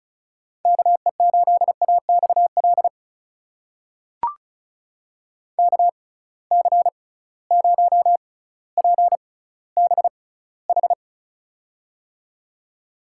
Morse code, heard as a keyed tone, about 36 wpm, spaced as for 18 wpm, 700 Hz. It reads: KE8AXL KC0PBH
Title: USA Call Signs - 6 Characters - 6x Character Spacing - 35wpm